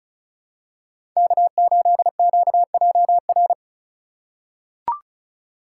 Morse code, heard as a keyed tone, about 36 wpm, 700 Hz. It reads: K8QJR
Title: USA Call Signs - 5 Characters - 2x Character Spacing - 35wpm